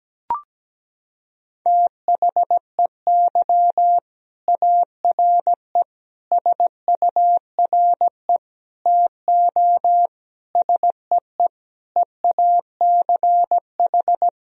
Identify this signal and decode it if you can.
Morse code, heard as a keyed tone, about 17 wpm, 700 Hz. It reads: THEY ARE SURE TO SEE EACH
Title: Sentences from Top 1000 Words - Review Part 1 - 17wpm